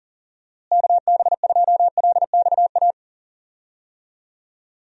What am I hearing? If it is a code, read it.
Morse code, heard as a keyed tone, about 40 wpm, 700 Hz. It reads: KB2LXA
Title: USA Call Signs - 6 Characters - Repeated 3x - 40wpm